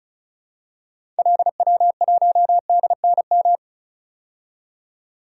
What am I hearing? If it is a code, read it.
Morse code, heard as a keyed tone, about 36 wpm, 700 Hz. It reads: LW1DNM
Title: DXCC-100 Call Signs - 6 Characters - 35wpm